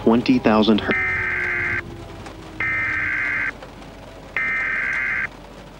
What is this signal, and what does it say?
Emergency Alert System (SAME) tones (header bursts), heard as harsh buzzing bursts, with speech. It is heard from 0.9 to 5.3 s.